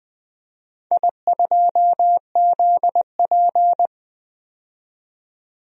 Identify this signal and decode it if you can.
Morse code, heard as a keyed tone, about 20 wpm, 700 Hz. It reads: I2ZP
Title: DXCC-100 Call Signs - 4 Characters - 20wpm